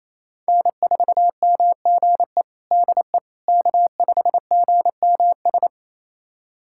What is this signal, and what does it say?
Morse code, heard as a keyed tone, about 27 wpm, 700 Hz. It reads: N4MGE DE K5GMS
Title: USA Callsign DE Callsign - 5 Characters - 28wpm